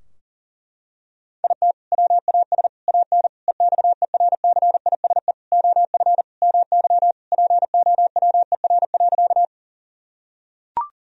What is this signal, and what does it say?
Morse code, heard as a keyed tone, about 40 wpm, 700 Hz. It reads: IT WAS AN EXERCISE OF MY POWER.
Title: Sentences from Top 1000 Words - Review Part 4 - 40wpm